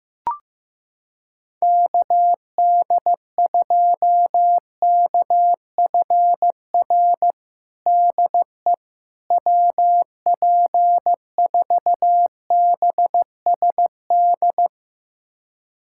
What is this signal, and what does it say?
Morse code, heard as a keyed tone, about 15 wpm, 700 Hz. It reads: KD2KFR DE WP4BSD
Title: USA Callsign DE Callsign 15wpm